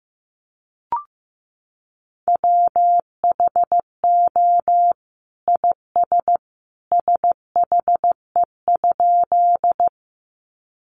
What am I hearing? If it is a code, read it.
Morse code, heard as a keyed tone, about 15 wpm, 700 Hz. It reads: WHO IS SHE?
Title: Sentences from Top 100 Words 15wpm